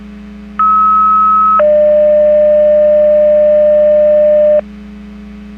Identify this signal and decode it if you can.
Two-tone page, heard as a one-second tone, then a three-second tone, about 1290 then 600 Hz. It is heard from 0.6 to 4.6 s.
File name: Portage 154.385 Paging